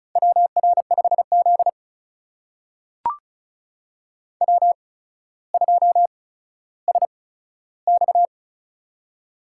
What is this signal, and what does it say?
Morse code, heard as a keyed tone, about 34 wpm, spaced as for 15 wpm, 700 Hz. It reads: WR5Z W2SX
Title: USA Call Signs - 4 Characters - 8x Character Spacing - 35wpm